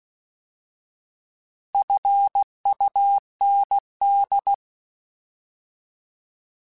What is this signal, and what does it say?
Morse code, heard as a keyed tone, about 16 wpm, 800 Hz. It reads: FUND
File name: Alphabetical / Alphabetical mp3s / Fs @15